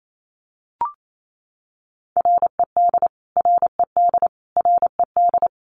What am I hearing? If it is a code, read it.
Morse code, heard as a keyed tone, about 28 wpm, 700 Hz. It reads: RED RED RED
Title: Top 500 Words Repeated 28wpm